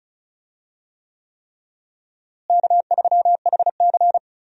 Morse code, heard as a keyed tone, about 36 wpm, 700 Hz. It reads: K3HC
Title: USA Call Signs - 4 Characters - Repeated 3x - 35wpm